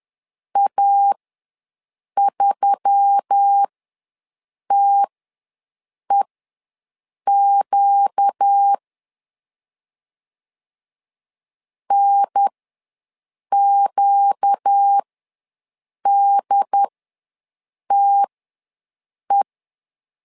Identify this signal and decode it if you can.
Morse code, heard as a keyed tone, about 10 wpm, spaced as for 6 wpm, 800 Hz. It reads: A3TEQ NQDTE
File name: LEZIONI / 4 Q1S3 ws 10wpm 2437